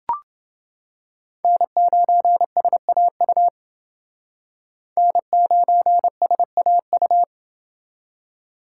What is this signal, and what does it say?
Morse code, heard as a keyed tone, about 30 wpm, 700 Hz. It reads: N9SAU N9SAU